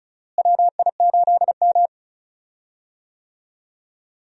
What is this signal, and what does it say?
Morse code, heard as a keyed tone, about 36 wpm, 700 Hz. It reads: WI8M
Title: USA Call Signs - 4 Characters - Repeated 2x - 35wpm